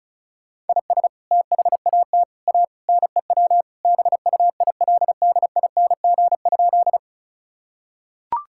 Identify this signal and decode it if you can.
Morse code, heard as a keyed tone, about 34 wpm, 700 Hz. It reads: IS THAT A NEW BUILDING?
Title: Sentences from Top 1000 Words - Review Part 5 - 35wpm